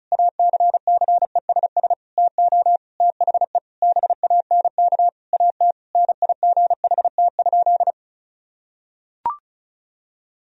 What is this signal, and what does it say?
Morse code, heard as a keyed tone, about 36 wpm, 700 Hz. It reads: ACCESS TO THE BANK AT NIGHT?
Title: Sentences from Top 1000 Words - Review Part 4 - 35wpm